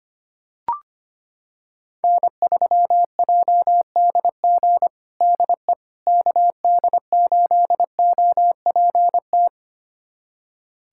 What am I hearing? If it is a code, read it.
Morse code, heard as a keyed tone, about 25 wpm, 700 Hz. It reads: N3JDG DE KD8OPT
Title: USA Callsign DE Callsign 25wpm